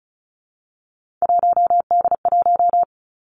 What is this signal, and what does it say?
Morse code, heard as a keyed tone, about 35 wpm, 700 Hz. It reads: 1D1
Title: Three Challenging Letter-Number Combinations 35wpm